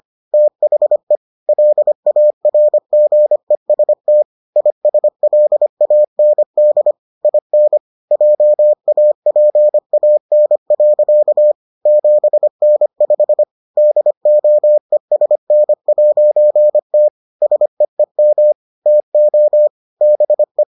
Morse code, heard as a keyed tone, about 25 wpm, 600 Hz. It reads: THE LARGEST ISLAND IN JAPAN. 7N5 DOESN'T SEEM TO BE